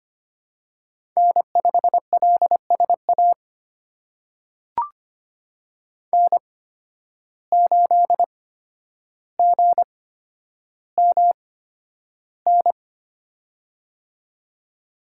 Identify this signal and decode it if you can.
Morse code, heard as a keyed tone, about 25 wpm, spaced as for 11 wpm, 700 Hz. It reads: N5LSA N8GMN
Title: USA Call Signs - 5 Characters - 8x Character Spacing - 25wpm